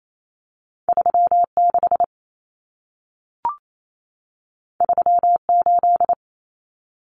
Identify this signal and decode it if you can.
Morse code, heard as a keyed tone, about 28 wpm, 700 Hz. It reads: 36 38
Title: Common Cut Numbers - 2-Digit 28wpm